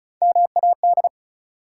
Morse code, heard as a keyed tone, about 36 wpm, 700 Hz. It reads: MAD
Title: Three Letter Words - 35wpm